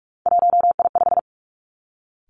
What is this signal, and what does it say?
Morse code, heard as a keyed tone, about 46 wpm, 700 Hz. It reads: 1I5
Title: Three Challenging Letter-Number Combinations 45wpm